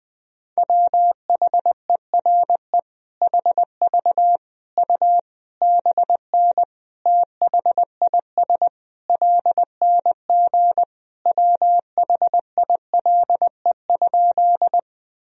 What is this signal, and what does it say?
Morse code, heard as a keyed tone, about 21 wpm, 700 Hz. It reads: WHERE HV U BN THIS LNG WHILE?
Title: Sentences from Top 300 Words and Abbreviations 20wpm